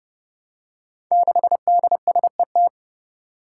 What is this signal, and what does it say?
Morse code, heard as a keyed tone, about 32 wpm, 700 Hz. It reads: 6DSET
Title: Five Challenging Letter-Number Combinations 30wpm